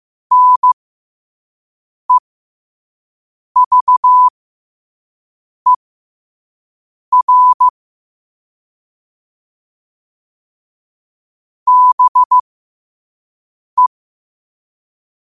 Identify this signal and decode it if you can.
Morse code, heard as a keyed tone, about 15 wpm, spaced as for 5 wpm, 1000 Hz. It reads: NEVER BE
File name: qotdfiles / 5wpm